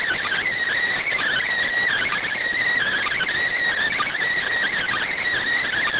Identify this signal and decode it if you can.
Emergency Alert System (SAME) tones (header bursts), heard as harsh buzzing bursts. It is heard from 0.9 to 5.3 s.